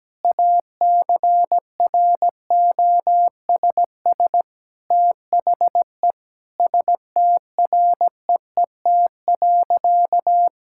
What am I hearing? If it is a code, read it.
Morse code, heard as a keyed tone, about 17 wpm, 700 Hz. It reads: ACROSS THE STREET.
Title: Sentences from Top 1000 Words - Review Part 5 - 17wpm